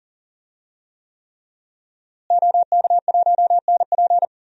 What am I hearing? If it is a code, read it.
Morse code, heard as a keyed tone, about 40 wpm, 700 Hz. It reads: OK1NP
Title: DXCC-100 Call Signs - 5 Characters - Repeated 3x - 40wpm